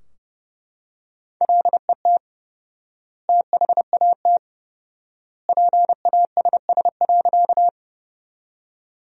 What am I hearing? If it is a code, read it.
Morse code, heard as a keyed tone, about 32 wpm, 700 Hz. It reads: LET THAT PASS.
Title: 4x Word-Spacing - Sentences from Top 400 Words - 30wpm